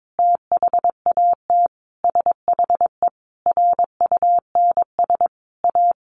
Morse code, heard as a keyed tone, about 22 wpm, 700 Hz. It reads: THAT SHE RUNS A
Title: Sentences from Top 600 Words 22wpm